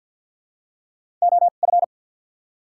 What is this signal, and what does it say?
Morse code, heard as a keyed tone, about 54 wpm, spaced as for 37 wpm, 700 Hz. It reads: OF
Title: 2x Character Spacing - Top 100 Words - 50wpm